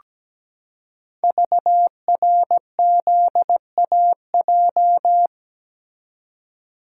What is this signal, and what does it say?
Morse code, heard as a keyed tone, about 17 wpm, 700 Hz. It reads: VRZAJ
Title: Code Groups - 5 Letters - 17wpm